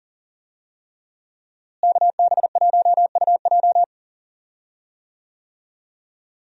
Morse code, heard as a keyed tone, about 38 wpm, 700 Hz. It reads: KB1UJ
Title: USA Call Signs - 5 Characters - Repeated 3x - 40wpm